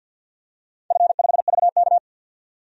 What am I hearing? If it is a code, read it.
Morse code, heard as a keyed tone, about 51 wpm, 700 Hz. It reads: F5VK
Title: DXCC-100 Call Signs - 4 Characters - 50wpm